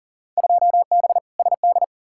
Morse code, heard as a keyed tone, about 40 wpm, 700 Hz. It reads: 2B SD
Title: Field Day Exchanges Encore - 40wpm